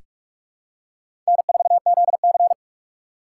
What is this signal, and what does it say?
Morse code, heard as a keyed tone, about 46 wpm, 700 Hz. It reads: N4ZC
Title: USA Call Signs - 2x Character Spacing - 45wpm